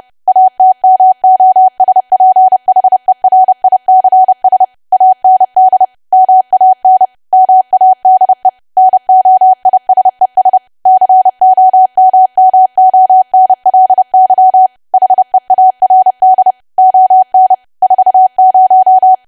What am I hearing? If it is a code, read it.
Morse code, heard as a keyed tone, about 30 wpm, 750 Hz. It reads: ATMOSPHERICS AND MAN MADE NOISES COMMONLY HEARD ON 40
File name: cwa / W1AW-Level3 / 30WPM